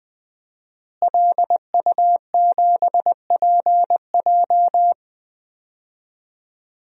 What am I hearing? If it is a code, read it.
Morse code, heard as a keyed tone, about 21 wpm, 700 Hz. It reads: LU7PJ
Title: DXCC-100 Call Signs - 5 Characters - Repeated 3x - 20wpm